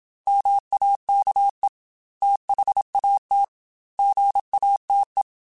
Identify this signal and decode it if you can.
Morse code, heard as a keyed tone, about 27 wpm, 800 Hz. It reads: MAKE THAT GATE